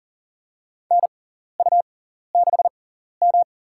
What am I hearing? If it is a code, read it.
Morse code, heard as a keyed tone, about 40 wpm, spaced as for 14 wpm, 700 Hz. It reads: NU6M
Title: USA Call Signs - 4 Characters - 6x Character Spacing - 40wpm